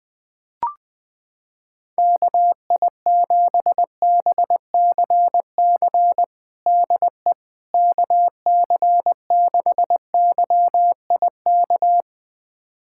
Morse code, heard as a keyed tone, about 20 wpm, 700 Hz. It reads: KI7BCC DE KC6YIK
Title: USA Callsign DE Callsign 20wpm